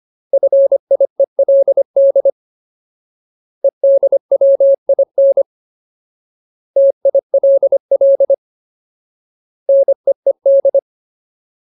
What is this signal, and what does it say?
Morse code, heard as a keyed tone, about 25 wpm, 550 Hz. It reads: FIELD EDWIN TILL NEED
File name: Ultimate Word List / 25WPM Ultimate